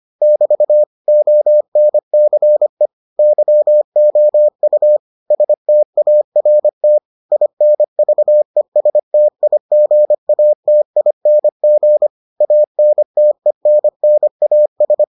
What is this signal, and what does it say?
Morse code, heard as a keyed tone, about 25 wpm, 600 Hz. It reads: <BT> ONCE YOU START INVESTIGATING ANTENNAS